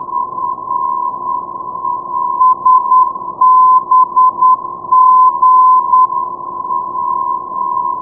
Morse code, heard as a keyed tone, about 9 wpm, 1020 Hz. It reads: SBG
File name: europe / aut / aut-40 ndb salzburg sbg 382.0 (+ fw-382) dkb 2009